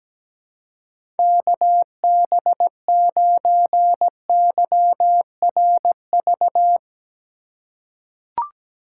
Morse code, heard as a keyed tone, about 17 wpm, 700 Hz. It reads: KB9YRV
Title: USA Call Signs - 6 Characters - 3x Character Spacing - 17wpm